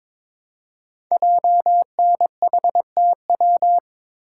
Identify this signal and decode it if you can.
Morse code, heard as a keyed tone, about 22 wpm, 700 Hz. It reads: JNHTW
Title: Code Groups - 5 Letters - 22wpm